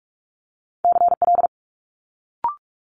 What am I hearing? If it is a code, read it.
Morse code, heard as a keyed tone, about 46 wpm, 700 Hz. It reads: CL
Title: Multi-letter Phonemes Quick Fire 45wpm